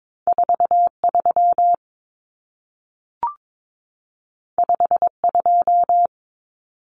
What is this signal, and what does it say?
Morse code, heard as a keyed tone, about 22 wpm, 700 Hz. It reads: 43 52
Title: Numbers - 2-Digit 22wpm